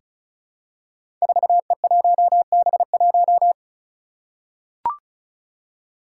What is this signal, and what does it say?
Morse code, heard as a keyed tone, about 36 wpm, 700 Hz. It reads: 4E1B1
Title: Five Challenging Letter-Number Combinations 35wpm